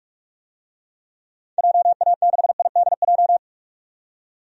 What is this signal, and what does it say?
Morse code, heard as a keyed tone, about 43 wpm, 700 Hz. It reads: JA6IDJ
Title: DXCC-100 Call Signs - 6 Characters - 45wpm